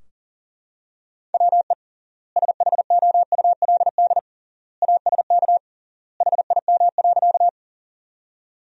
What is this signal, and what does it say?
Morse code, heard as a keyed tone, about 40 wpm, 700 Hz. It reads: WE SHOULD ASK HIM.
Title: 3x Word-Spacing - Sentences from Top 300 Words - 40wpm